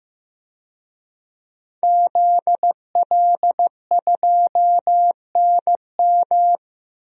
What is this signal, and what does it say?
Morse code, heard as a keyed tone, about 15 wpm, 700 Hz. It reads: ZL2NM
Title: DXCC-100 Call Signs - 5 Characters - 15wpm